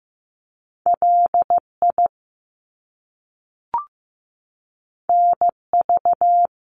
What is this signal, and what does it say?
Morse code, heard as a keyed tone, about 15 wpm, 700 Hz. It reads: LI NV
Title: Top Two Letter Combinations in Words 15wpm